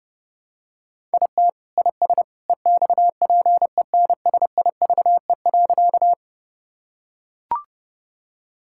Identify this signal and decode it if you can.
Morse code, heard as a keyed tone, about 30 wpm, 700 Hz. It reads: IT IS EXPENSIVE.
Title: Sentences from Top 1000 Words - Review Part 6 - 30wpm